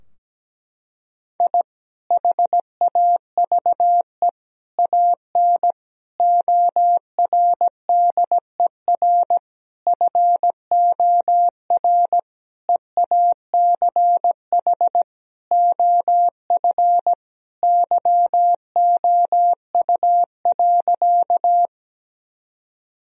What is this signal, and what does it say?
Morse code, heard as a keyed tone, about 17 wpm, 700 Hz. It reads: I HAVE AN ORDER FOR EACH OF YOU.
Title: Sentences from Top 1000 Words - Review Part 2 - 17wpm